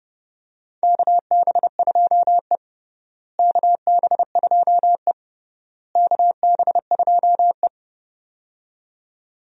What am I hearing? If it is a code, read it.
Morse code, heard as a keyed tone, about 30 wpm, 700 Hz. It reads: KB2E KB2E KB2E
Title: USA Call Signs - 4 Characters - Repeated 3x - 30wpm